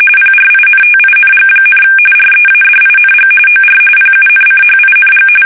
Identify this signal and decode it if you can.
RTTY, about 45.45 baud. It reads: PRINTING, READING OR PUNCHING
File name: fsk850 / teletype-corp